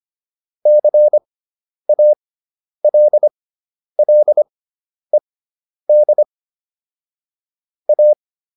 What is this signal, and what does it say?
Morse code, heard as a keyed tone, about 25 wpm, spaced as for 10 wpm, 600 Hz. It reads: CALLED A